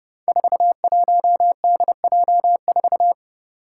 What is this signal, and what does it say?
Morse code, heard as a keyed tone, about 31 wpm, 700 Hz. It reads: 41DJ4
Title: Five Challenging Letter-Number Combinations 30wpm